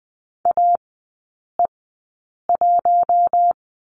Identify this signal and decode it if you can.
Morse code, heard as a keyed tone, about 20 wpm, spaced as for 8 wpm, 700 Hz. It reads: AE1
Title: 041 Lesson 8e - Number 1 - Three Characters 20wpm